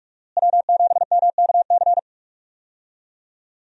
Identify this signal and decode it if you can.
Morse code, heard as a keyed tone, about 46 wpm, 700 Hz. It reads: W7MKC
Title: USA Call Signs - 5 Characters - Repeated 3x - 45wpm